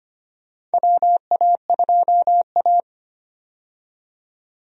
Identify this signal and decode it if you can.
Morse code, heard as a keyed tone, about 25 wpm, 700 Hz. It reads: WA2A
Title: USA Call Signs - 4 Characters - 25wpm